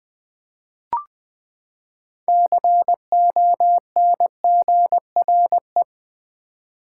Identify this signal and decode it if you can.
Morse code, heard as a keyed tone, about 20 wpm, 700 Hz. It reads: CONGRE
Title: Build A Word - 6 Letters At A Time - 12-Letter Words - 20wpm